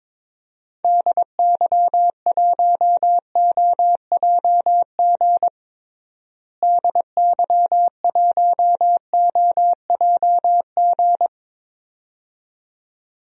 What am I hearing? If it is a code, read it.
Morse code, heard as a keyed tone, about 21 wpm, 700 Hz. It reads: DY1OJG DY1OJG
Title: DXCC-100 Call Signs - 6 Characters - Repeated 2x - 22wpm